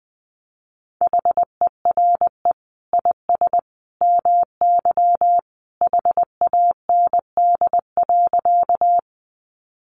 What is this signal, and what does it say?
Morse code, heard as a keyed tone, about 20 wpm, 700 Hz. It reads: HERE IS MY HAND.
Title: Sentences from Top 200 Words 20wpm